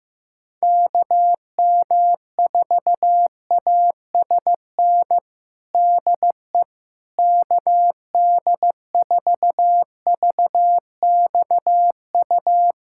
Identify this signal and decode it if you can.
Morse code, heard as a keyed tone, about 16 wpm, 700 Hz. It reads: KM4ASN DE KD4VXU
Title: USA Callsign DE Callsign - 6 Characters - 15wpm